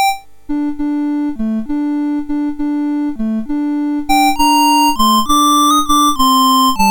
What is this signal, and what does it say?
There is no code.